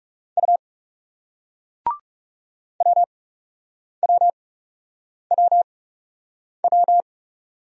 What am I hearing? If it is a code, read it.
Morse code, heard as a keyed tone, about 43 wpm, spaced as for 8 wpm, 700 Hz. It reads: U WWWW